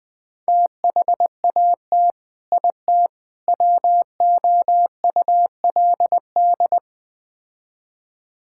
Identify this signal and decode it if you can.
Morse code, heard as a keyed tone, about 20 wpm, 700 Hz. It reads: THAT IT WOULD
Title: Build A Phrase - 4-Word Phrases - 20wpm